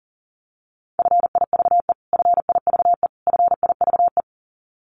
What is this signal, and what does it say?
Morse code, heard as a keyed tone, about 40 wpm, 700 Hz. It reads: FIVE FIVE FIVE
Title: Top 500 Words Repeated 40wpm